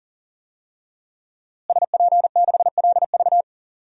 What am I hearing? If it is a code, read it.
Morse code, heard as a keyed tone, about 40 wpm, 700 Hz. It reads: SP6LV
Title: DXCC-100 Call Signs - 5 Characters - Repeated 2x - 40wpm